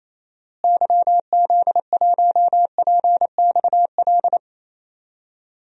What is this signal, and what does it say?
Morse code, heard as a keyed tone, about 28 wpm, 700 Hz. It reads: YZ1PXL
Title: DXCC-100 Call Signs - 6 Characters - Repeated 3x - 28wpm